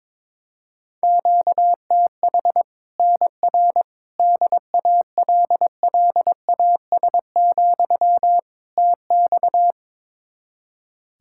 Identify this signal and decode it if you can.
Morse code, heard as a keyed tone, about 22 wpm, 700 Hz. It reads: QTH NR DALLAS, TX
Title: Practice QSO - Exchange 1 - Beginning - 22wpm